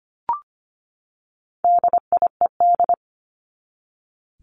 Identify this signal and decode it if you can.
Morse code, heard as a keyed tone, about 25 wpm, 700 Hz. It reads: DIED
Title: ED Suffix - Top 100 Words with Base Words 25wpm